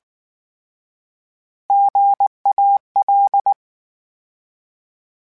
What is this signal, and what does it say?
Morse code, heard as a keyed tone, about 19 wpm, 800 Hz. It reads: GAL